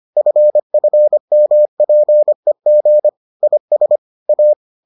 Morse code, heard as a keyed tone, about 25 wpm, 600 Hz. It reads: FFMPEG IS A